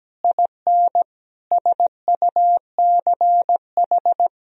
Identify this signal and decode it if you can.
Morse code, heard as a keyed tone, about 17 wpm, 700 Hz. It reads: IN SUCH
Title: Sentences from Top 1000 Words - Review Part 2 - 17wpm